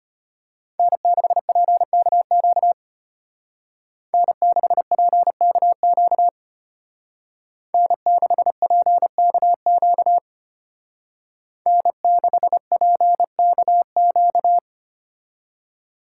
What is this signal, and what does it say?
Morse code, heard as a keyed tone, about 33 wpm, 700 Hz. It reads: N6PKQ N6PKQ N6PKQ N6PKQ